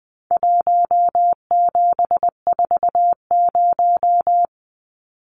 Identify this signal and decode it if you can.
Morse code, heard as a keyed tone, about 19 wpm, 700 Hz. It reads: 1740
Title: Numbers - 4-Digit 20wpm